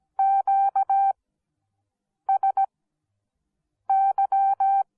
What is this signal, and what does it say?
Morse code, heard as a keyed tone, about 17 wpm, spaced as for 6 wpm, 780 Hz. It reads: QSY